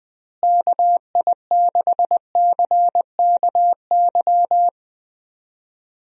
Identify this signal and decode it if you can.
Morse code, heard as a keyed tone, about 20 wpm, 700 Hz. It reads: KI6CKY
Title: USA Call Signs - 6 Characters - Repeated 3x - 20wpm